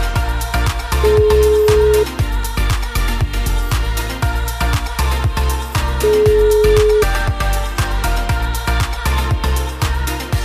Ringback tone (Europe, 425 Hz), with noise.